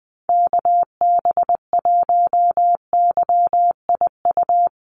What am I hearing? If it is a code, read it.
Morse code, heard as a keyed tone, about 20 wpm, 700 Hz. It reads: KB1YIU
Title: USA Call Signs 20wpm